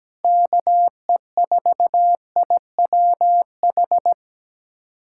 Morse code, heard as a keyed tone, about 17 wpm, 700 Hz. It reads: KE4IWH
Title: USA Callsign DE Callsign - 6 Characters - 17wpm